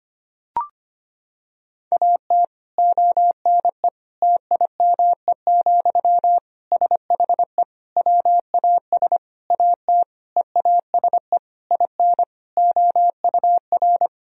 Morse code, heard as a keyed tone, about 25 wpm, 700 Hz. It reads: AT ONE TIME, SHE WAS AT EASE IN OUR
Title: Sentences from Top 1000 Words - Review Part 1 - 25wpm